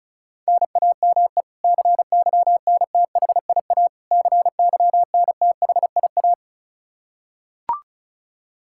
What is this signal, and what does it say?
Morse code, heard as a keyed tone, about 34 wpm, 700 Hz. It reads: NAME CYNTHIA CYNTHIA
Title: K1USN SST QSO Practice 35wpm